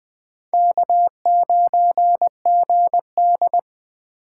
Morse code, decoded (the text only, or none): K9GD